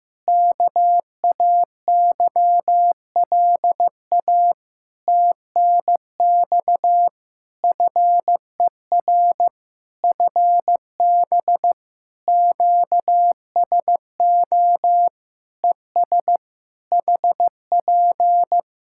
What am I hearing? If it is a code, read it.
Morse code, heard as a keyed tone, about 15 wpm, 700 Hz. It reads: KAYLA TNX FER FB QSO ES HP
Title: K1USN SST QSO Practice 15wpm